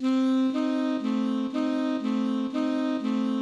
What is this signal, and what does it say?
There is no code.